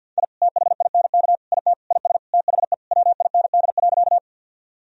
Morse code, heard as a keyed tone, about 52 wpm, 700 Hz. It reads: I THINK IT IS THE WIND.
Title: Sentences from Top 1000 Words - Review Part 2 - 50wpm